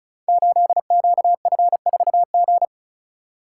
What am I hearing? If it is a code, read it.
Morse code, heard as a keyed tone, about 36 wpm, 700 Hz. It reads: 8QF4G
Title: Code Groups - 5 Letter:Numbers - 35wpm